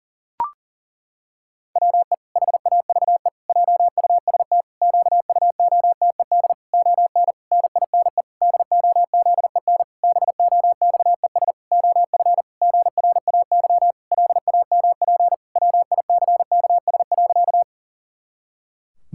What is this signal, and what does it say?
Morse code, heard as a keyed tone, about 40 wpm, 700 Hz. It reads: WE HAVE JUST QUOTED ON NINE DOZEN BOXES OF GRAY LAMP WICKS.
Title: Pangram Challenge v2 - 40wpm